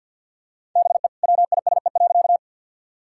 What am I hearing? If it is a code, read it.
Morse code, heard as a keyed tone, about 50 wpm, 700 Hz. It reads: BE WISE.
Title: Sentences from Top 1600 Words 50wpm